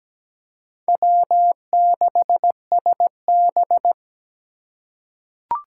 Morse code, heard as a keyed tone, about 17 wpm, 700 Hz. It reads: W6SB
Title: USA Call Signs - 4 Characters - 2x Character Spacing - 17wpm